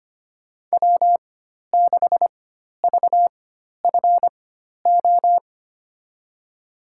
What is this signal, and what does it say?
Morse code, heard as a keyed tone, about 25 wpm, spaced as for 12 wpm, 700 Hz. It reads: W6VFO
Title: USA Call Signs - 5 Characters - 4x Character Spacing - 25wpm